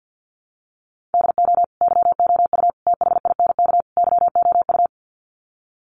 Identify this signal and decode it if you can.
Morse code, heard as a keyed tone, about 50 wpm, 700 Hz. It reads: DO YOU THINK YOU
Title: Sets of 4 Words 50wpm